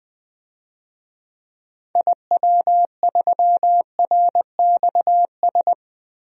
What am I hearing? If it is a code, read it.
Morse code, heard as a keyed tone, about 20 wpm, 700 Hz. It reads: IW3RXS
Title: DXCC-100 Call Signs - 6 Characters - Repeated 3x - 20wpm